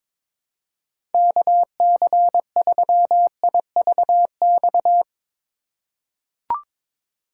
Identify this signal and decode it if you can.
Morse code, heard as a keyed tone, about 22 wpm, 700 Hz. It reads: KC3IVX
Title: USA Call Signs - 6 Characters - 2x Character Spacing - 22wpm